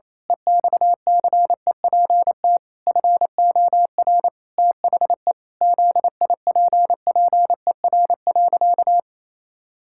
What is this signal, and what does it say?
Morse code, heard as a keyed tone, about 27 wpm, 700 Hz. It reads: EXCEPT FOR THE ZIPPER.
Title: Pangram Challenge v2 - 28wpm